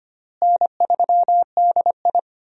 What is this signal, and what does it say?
Morse code, heard as a keyed tone, about 25 wpm, 700 Hz. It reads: N3DI
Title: USA Call Signs - 4 Characters - Repeated 3x - 25wpm